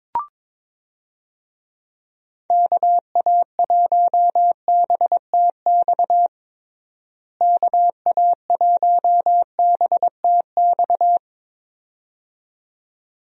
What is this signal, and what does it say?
Morse code, heard as a keyed tone, about 22 wpm, 700 Hz. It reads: KA1BTX KA1BTX
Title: USA Call Signs - 6 Characters - Repeated 2x - 22wpm